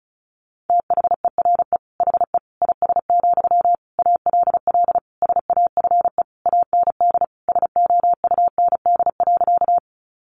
Morse code, heard as a keyed tone, about 34 wpm, 700 Hz. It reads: THERE HE IS, ALL SAFE AND SOUND.
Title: Sentences from Top 900 Words 35wpm